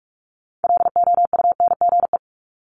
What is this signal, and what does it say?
Morse code, heard as a keyed tone, about 46 wpm, 700 Hz. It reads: LOUNGE
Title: Multi-letter Phonemes 45wpm